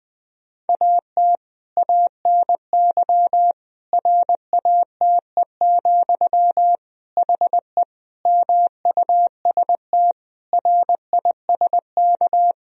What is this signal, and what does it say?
Morse code, heard as a keyed tone, about 20 wpm, 700 Hz. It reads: AT ANY RATE, HE MUST RISK
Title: Sentences from Top 1000 Words - Review Part 4 - 20wpm